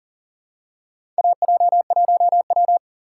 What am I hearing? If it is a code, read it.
Morse code, heard as a keyed tone, about 38 wpm, 700 Hz. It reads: AJ1W
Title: USA Call Signs - 4 Characters - 6x Character Spacing - 40wpm